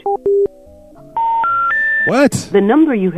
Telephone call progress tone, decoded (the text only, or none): special information tone